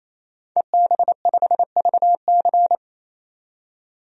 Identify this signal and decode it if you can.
Morse code, heard as a keyed tone, about 28 wpm, 700 Hz. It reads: EB5VC
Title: DXCC-100 Call Signs - 5 Characters - 28wpm